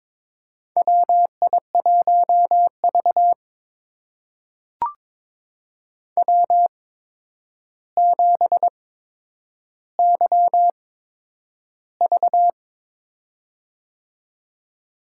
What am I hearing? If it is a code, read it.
Morse code, heard as a keyed tone, about 22 wpm, spaced as for 9 wpm, 700 Hz. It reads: WI1V W7YV
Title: USA Call Signs - 4 Characters - 8x Character Spacing - 22wpm